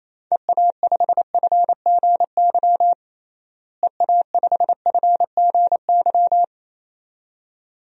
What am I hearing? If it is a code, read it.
Morse code, heard as a keyed tone, about 28 wpm, 700 Hz. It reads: EA5FGY EA5FGY
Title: DXCC-100 Call Signs - 6 Characters - Repeated 2x - 28wpm